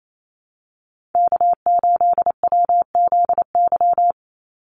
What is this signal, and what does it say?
Morse code, heard as a keyed tone, about 28 wpm, 700 Hz. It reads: K8WZY
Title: USA Call Signs 28wpm